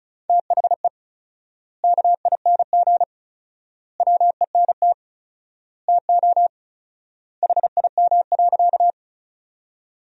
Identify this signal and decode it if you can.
Morse code, heard as a keyed tone, about 36 wpm, 700 Hz. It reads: THE KING WENT TO HIM.
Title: 4x Word-Spacing - Sentences from Top 400 Words - 35wpm